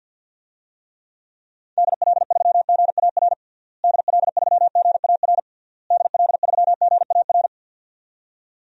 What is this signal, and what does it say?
Morse code, heard as a keyed tone, about 50 wpm, 700 Hz. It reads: DL3GAR DL3GAR DL3GAR
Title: DXCC-100 Call Signs - 6 Characters - Repeated 3x - 50wpm